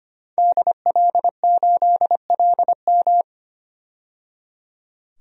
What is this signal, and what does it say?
Morse code, heard as a keyed tone, about 25 wpm, 700 Hz. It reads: DL8LM
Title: DXCC-100 Call Signs - 5 Characters - 25wpm